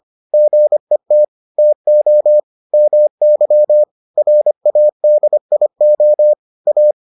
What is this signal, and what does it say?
Morse code, heard as a keyed tone, about 25 wpm, 600 Hz. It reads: GET TO MY RADIO A